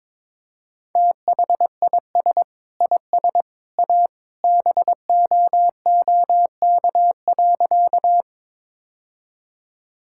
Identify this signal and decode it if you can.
Morse code, heard as a keyed tone, about 22 wpm, 700 Hz. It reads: THIS IS A BOOK.
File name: practice / 22wpm / Sentences with 3-Word Phrases - 22wpm